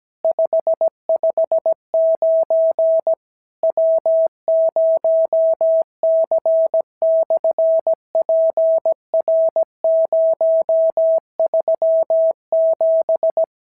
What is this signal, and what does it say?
Morse code, heard as a keyed tone, about 17 wpm, 650 Hz. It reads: 559 W0C/PR037
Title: SOTA QSO Practice 17wpm